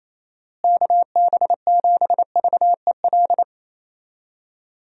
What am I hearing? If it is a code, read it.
Morse code, heard as a keyed tone, about 28 wpm, 700 Hz. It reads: KB7VEL
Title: USA Call Signs - 6 Characters - Repeated 3x - 28wpm